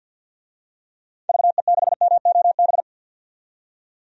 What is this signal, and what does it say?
Morse code, heard as a keyed tone, about 48 wpm, 700 Hz. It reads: VE6MOB
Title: DXCC-100 Call Signs - 6 Characters - 50wpm